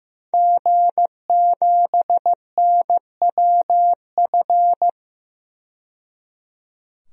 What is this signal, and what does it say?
Morse code, heard as a keyed tone, about 16 wpm, 700 Hz. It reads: G7NWF
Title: DXCC-100 Call Signs - 5 Characters - 15wpm